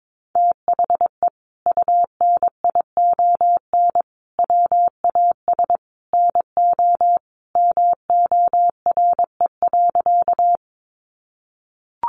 Morse code, heard as a keyed tone, about 22 wpm, 700 Hz. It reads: THE UNION WAS NO MORE.
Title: Sentences from Top 1400 Words 22wpm